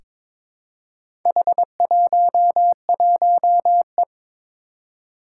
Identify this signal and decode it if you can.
Morse code, heard as a keyed tone, about 22 wpm, 700 Hz. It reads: H11E
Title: Four Challenging Letter-Number Combinations 22wpm